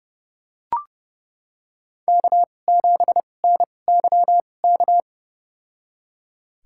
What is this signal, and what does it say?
Morse code, heard as a keyed tone, about 32 wpm, spaced as for 22 wpm, 700 Hz. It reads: K7NYK
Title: USA Call Signs - 5 Characters - 2x Character Spacing - 30wpm